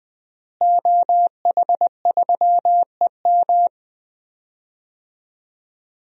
Morse code, heard as a keyed tone, about 20 wpm, 700 Hz. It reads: OH3EM
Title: DXCC-100 Call Signs - 5 Characters - Repeated 3x - 20wpm